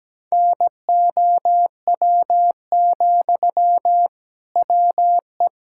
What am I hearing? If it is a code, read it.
Morse code, heard as a keyed tone, about 17 wpm, 700 Hz. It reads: NOW, WE
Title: Sentences from Top 1000 Words - Review Part 6 - 17wpm